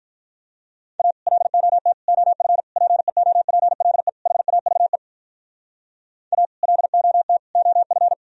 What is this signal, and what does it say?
Morse code, heard as a keyed tone, about 46 wpm, 700 Hz. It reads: A LOT OF PEOPLE HAVE A LOT OF